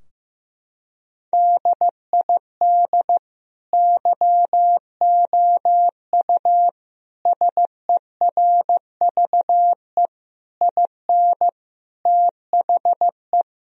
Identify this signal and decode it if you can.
Morse code, heard as a keyed tone, about 15 wpm, 700 Hz. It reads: DID YOU SERVE IN THE
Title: Sentences from Top 1000 Words - Review Part 4 - 15wpm